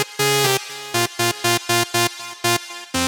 There is no code.